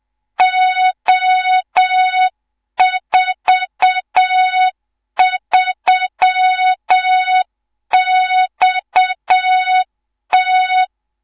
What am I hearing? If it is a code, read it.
Morse code, heard as a keyed tone, about 7 wpm, 3700 Hz, with noise. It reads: O43XT
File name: Morze-HA2MN / hang / 18 IRASJELEK2